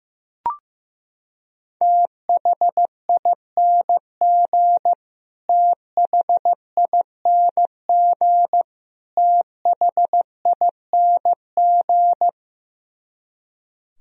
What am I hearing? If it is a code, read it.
Morse code, heard as a keyed tone, about 15 wpm, 700 Hz. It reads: THING THING THING
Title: Top 100 Words Repeated 15wpm